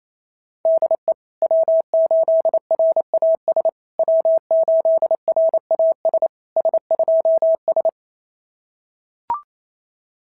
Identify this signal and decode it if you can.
Morse code, heard as a keyed tone, about 27 wpm, 690 Hz. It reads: DE W8RAS W8RAS S2S
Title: SOTA QSO Practice 28wpm